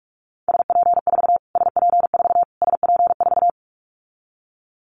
Morse code, heard as a keyed tone, about 44 wpm, 700 Hz. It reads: SP4 SP4 SP4
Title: Callsign Prefixes for Top 25 DXCC - Repeated 3x - 45wpm